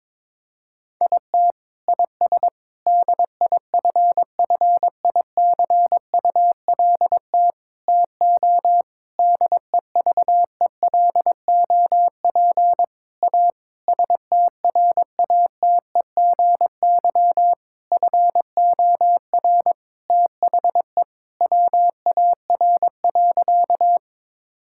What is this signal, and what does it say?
Morse code, heard as a keyed tone, about 22 wpm, 700 Hz. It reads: IT IS DIFFICULT TO DEVELOP A STRATEGY FOR THE WAR.
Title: Sentences from Top 1000 Words - Review Part 5 - 22wpm